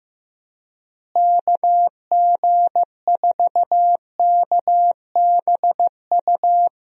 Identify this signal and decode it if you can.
Morse code, heard as a keyed tone, about 15 wpm, 700 Hz. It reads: KG4KBU
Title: USA Call Signs - 6 Characters - Repeated 2x - 15wpm